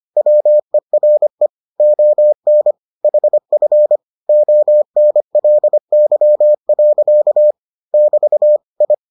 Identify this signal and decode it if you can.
Morse code, heard as a keyed tone, about 25 wpm, 600 Hz. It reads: WERE ON HF ONLY. <BT> I